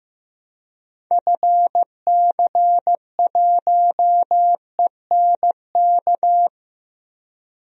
Morse code, heard as a keyed tone, about 15 wpm, 700 Hz. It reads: FC1ENK
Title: DXCC-100 Call Signs - 6 Characters - 15wpm